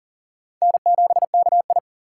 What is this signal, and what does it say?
Morse code, heard as a keyed tone, about 40 wpm, 700 Hz. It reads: N7KI
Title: USA Call Signs - 4 Characters - 6x Character Spacing - 40wpm